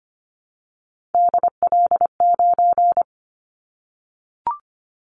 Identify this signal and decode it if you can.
Morse code, heard as a keyed tone, about 25 wpm, 700 Hz. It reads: DL9
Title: Callsign Prefixes for Top 25 DXCC - Repeated 3x - 25wpm